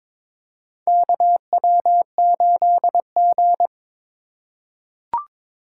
Morse code, heard as a keyed tone, about 22 wpm, 700 Hz. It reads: KW8G
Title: USA Call Signs - 4x Character Spacing - 22wpm